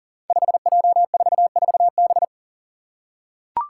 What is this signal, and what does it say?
Morse code, heard as a keyed tone, about 40 wpm, 700 Hz. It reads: 5J44B
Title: Five Challenging Letter-Number Combinations 40wpm